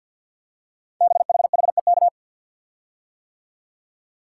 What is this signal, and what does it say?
Morse code, heard as a keyed tone, about 51 wpm, 700 Hz. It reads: BHHEK
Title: Code Groups - 5 Letters - 50wpm